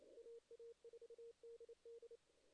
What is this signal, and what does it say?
Morse code, heard as a keyed tone, about 32 wpm, 440 Hz. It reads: AA4DD
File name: 2016 JAN CW NAQP / 2016 Jan naqp k5ot - 0-8